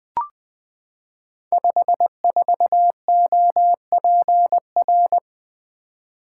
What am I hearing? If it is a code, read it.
Morse code, heard as a keyed tone, about 20 wpm, 700 Hz. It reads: 54OPR
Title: Code Groups - 5 Letter:Numbers - 20wpm